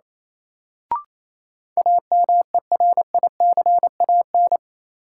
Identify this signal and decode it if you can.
Morse code, heard as a keyed tone, about 28 wpm, 700 Hz. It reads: AMERICAN
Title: Sets of 2 Words - New Word Drill - 28wpm